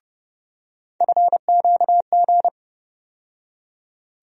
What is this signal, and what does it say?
Morse code, heard as a keyed tone, about 30 wpm, 700 Hz. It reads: FQG
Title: USA Call Sign Postfixes - 3 Letters 30wpm